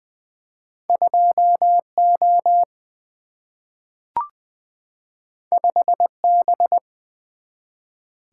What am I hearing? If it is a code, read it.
Morse code, heard as a keyed tone, about 20 wpm, 700 Hz. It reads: 2 O 5 B
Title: Single-Letter Single-Number 20wpm